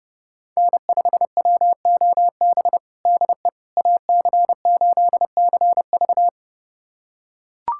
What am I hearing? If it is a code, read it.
Morse code, heard as a keyed tone, about 30 wpm, 700 Hz. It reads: N5WOB DE AC8CV
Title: USA Callsign DE Callsign - 5 Characters - 30wpm